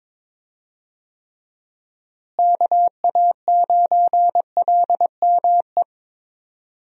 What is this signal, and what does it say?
Morse code, heard as a keyed tone, about 22 wpm, 700 Hz. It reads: KA9LME
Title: USA Call Signs - 6 Characters - Repeated 3x - 22wpm